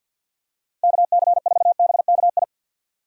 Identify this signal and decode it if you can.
Morse code, heard as a keyed tone, about 51 wpm, 700 Hz. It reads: KC4BKI
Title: USA Call Signs - 6 Characters - Repeated 3x - 50wpm